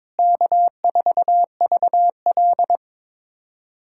Morse code, heard as a keyed tone, about 22 wpm, 700 Hz. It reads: K4VL
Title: USA Callsign DE Callsign - 4 Characters - 22wpm